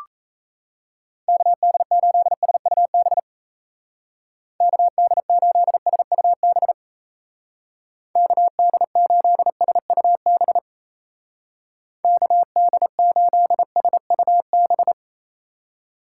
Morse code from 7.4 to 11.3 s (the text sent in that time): KD8SUB